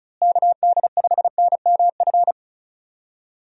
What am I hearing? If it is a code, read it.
Morse code, heard as a keyed tone, about 36 wpm, 700 Hz. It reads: KD5NMF
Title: USA Call Signs - 6 Characters - Repeated 3x - 35wpm